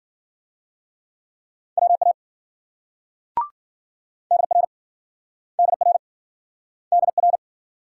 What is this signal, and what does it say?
Morse code, heard as a keyed tone, about 52 wpm, 700 Hz. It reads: WA DR DR DR